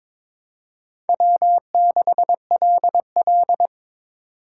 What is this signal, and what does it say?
Morse code, heard as a keyed tone, about 22 wpm, 700 Hz. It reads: W6LL
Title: USA Call Signs - 4 Characters - 3x Character Spacing - 22wpm